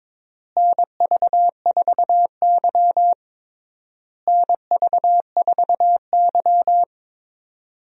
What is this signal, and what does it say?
Morse code, heard as a keyed tone, about 22 wpm, 700 Hz. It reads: NV4Y NV4Y
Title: USA Call Signs - 4 Characters - Repeated 3x - 22wpm